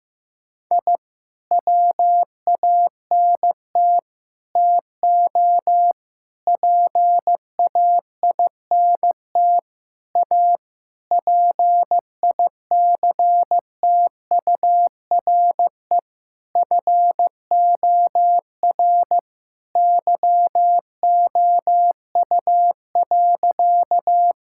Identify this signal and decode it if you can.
Morse code, heard as a keyed tone, about 15 wpm, 700 Hz. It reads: I WANT TO PAINT A PICTURE FOR YOU.
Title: Sentences from Top 1000 Words - Review Part 6 - 15wpm